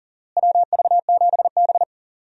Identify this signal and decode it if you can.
Morse code, heard as a keyed tone, about 40 wpm, 700 Hz. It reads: WV7B
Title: USA Call Signs - 4 Characters - Repeated 2x - 40wpm